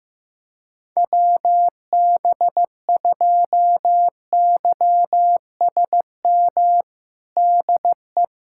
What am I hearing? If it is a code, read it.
Morse code, heard as a keyed tone, about 15 wpm, 700 Hz. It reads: WB2YSM DE
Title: USA Callsign DE Callsign 15wpm